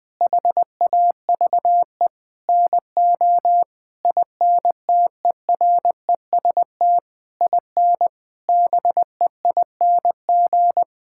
Morse code, heard as a keyed tone, about 21 wpm, 700 Hz. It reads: HAVE NO INTEREST IN BEING
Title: Sentences from Top 1000 Words - Review Part 6 - 20wpm